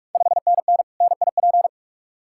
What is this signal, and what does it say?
Morse code, heard as a keyed tone, about 47 wpm, 690 Hz. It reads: 5NN NIP
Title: Ontario QSO Party Exchange 45wpm